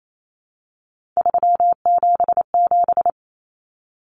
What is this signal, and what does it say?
Morse code, heard as a keyed tone, about 28 wpm, 700 Hz. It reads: 377
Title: Numbers - 3-Digit 28wpm